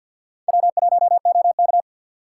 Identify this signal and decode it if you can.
Morse code, heard as a keyed tone, about 60 wpm, 700 Hz. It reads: W1OK
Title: Speed Racing - USA 1x2 Call Signs 50wpm